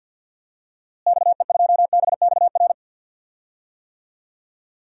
Morse code, heard as a keyed tone, about 52 wpm, 700 Hz. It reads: XE2BXR
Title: DXCC-100 Call Signs - 6 Characters - Repeated 3x - 50wpm